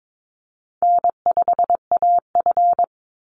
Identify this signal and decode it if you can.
Morse code, heard as a keyed tone, about 22 wpm, 700 Hz. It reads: N5AF
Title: Easy Call Signs Repeated 3x - 22wpm